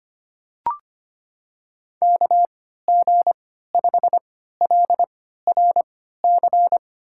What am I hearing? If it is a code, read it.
Morse code, heard as a keyed tone, about 25 wpm, spaced as for 14 wpm, 700 Hz. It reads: KG5LRC
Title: USA Call Signs - 6 Characters - 3x Character Spacing - 25wpm